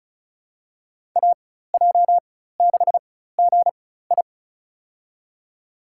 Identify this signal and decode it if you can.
Morse code, heard as a keyed tone, about 34 wpm, spaced as for 16 wpm, 700 Hz. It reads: AJ6GI
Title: USA Call Signs - 5 Characters - 4x Character Spacing - 35wpm